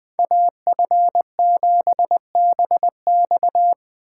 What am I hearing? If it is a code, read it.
Morse code, heard as a keyed tone, about 20 wpm, 700 Hz. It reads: AF7BX